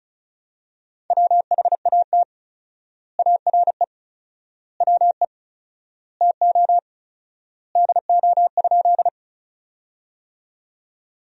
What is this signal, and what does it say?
Morse code, heard as a keyed tone, about 36 wpm, 700 Hz. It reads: WHAT ARE WE TO DO?
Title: 4x Word-Spacing - Sentences from Top 100 Words - 35wpm